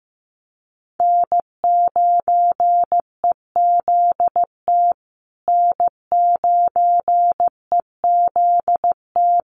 Morse code, heard as a keyed tone, about 15 wpm, 700 Hz. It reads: N9EZT N9EZT
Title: USA Call Signs Repeated 3x 15wpm